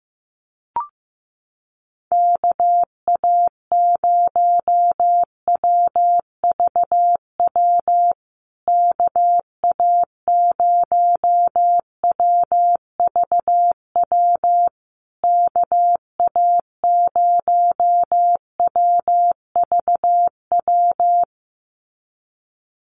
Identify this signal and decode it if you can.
Morse code, heard as a keyed tone, about 15 wpm, 700 Hz. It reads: KA0WVW KA0WVW KA0WVW
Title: USA Call Signs Repeated 3x 15wpm